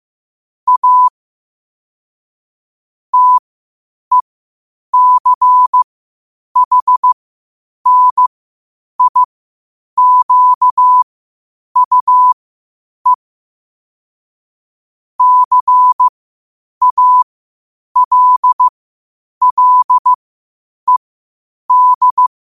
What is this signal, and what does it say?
Morse code, heard as a keyed tone, about 15 wpm, spaced as for 8 wpm, 1000 Hz. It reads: A TECHNIQUE CALLED
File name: qotdfiles / 8wpm